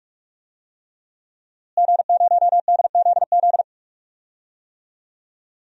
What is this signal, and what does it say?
Morse code, heard as a keyed tone, about 46 wpm, 700 Hz. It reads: G0DZZ
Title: DXCC-100 Call Signs - 5 Characters - Repeated 3x - 45wpm